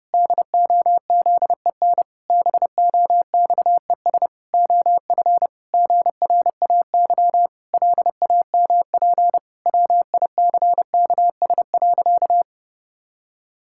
Morse code, heard as a keyed tone, about 30 wpm, 700 Hz. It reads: DOZEN BOXES OF GRAY LAMP WICKS.
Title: Pangram Challenge v2 - 30wpm